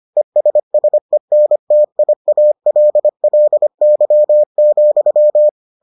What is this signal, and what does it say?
Morse code, heard as a keyed tone, about 25 wpm, 600 Hz. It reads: ESSENTIALLY,